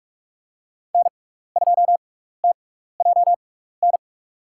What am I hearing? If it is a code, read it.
Morse code, heard as a keyed tone, about 44 wpm, spaced as for 15 wpm, 700 Hz. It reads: N2TJN
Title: USA Call Signs - 6x Character Spacing - 45wpm